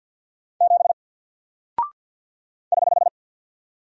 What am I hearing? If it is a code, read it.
Morse code, heard as a keyed tone, about 50 wpm, 700 Hz. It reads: 7 <HH>
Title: Single Character (Letter:Number:Punctuation:Prosign) Alt 50wpm